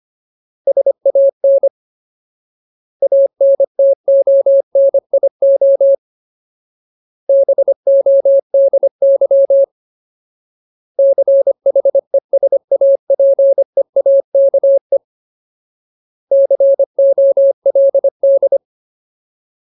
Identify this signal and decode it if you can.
Morse code, heard as a keyed tone, about 25 wpm, 550 Hz. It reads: SAN ANTONIO BODY CHESAPEAKE COLD
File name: Ultimate Word List / 25WPM Ultimate